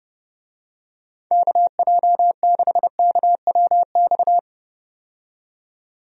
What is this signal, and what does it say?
Morse code, heard as a keyed tone, about 30 wpm, 700 Hz. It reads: KJ6KWX
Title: USA Call Signs - 6 Characters - 30wpm